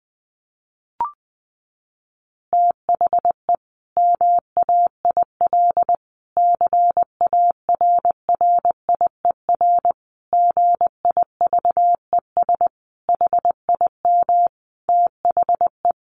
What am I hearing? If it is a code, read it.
Morse code, heard as a keyed tone, about 20 wpm, 700 Hz. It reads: THE MAIL CARRIER GIVES HIM THE
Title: Additional Easy Phrases 20wpm